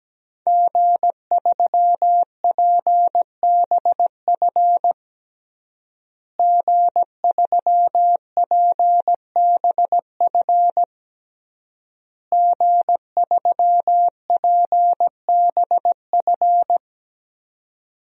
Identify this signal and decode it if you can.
Morse code, heard as a keyed tone, about 17 wpm, 700 Hz. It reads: G3PBF G3PBF G3PBF
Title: DXCC-100 Call Signs - 5 Characters - Repeated 3x - 17wpm